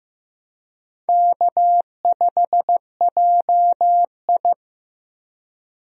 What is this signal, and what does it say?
Morse code, heard as a keyed tone, about 15 wpm, 700 Hz. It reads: K5JI
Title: USA Call Signs - 4 Characters - Repeated 3x - 15wpm